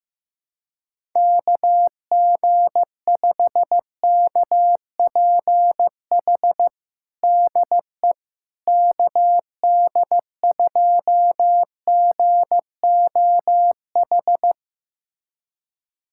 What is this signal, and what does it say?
Morse code, heard as a keyed tone, about 16 wpm, 700 Hz. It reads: KG5KPH DE KD2GOH
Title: USA Callsign DE Callsign 15wpm